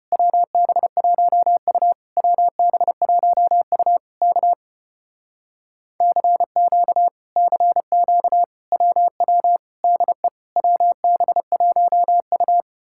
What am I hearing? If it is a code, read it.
Morse code, heard as a keyed tone, about 33 wpm, 700 Hz. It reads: WB1U WB1U K CQ CQ WW DE WB1U